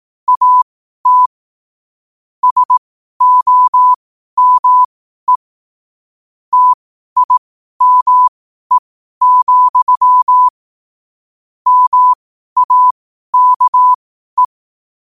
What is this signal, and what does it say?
Morse code, heard as a keyed tone, about 18 wpm, spaced as for 12 wpm, 1000 Hz. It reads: AT SOME TIME, MAKE
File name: qotdfiles / 12wpm